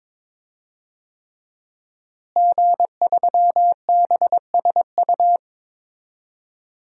Morse code, heard as a keyed tone, about 22 wpm, 700 Hz. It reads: G3BSU
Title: DXCC-100 Call Signs - 5 Characters - Repeated 3x - 22wpm